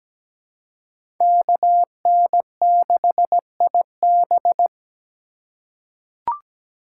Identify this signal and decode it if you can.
Morse code, heard as a keyed tone, about 17 wpm, 700 Hz. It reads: KN6IB
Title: USA Call Signs - 5 Characters - 3x Character Spacing - 17wpm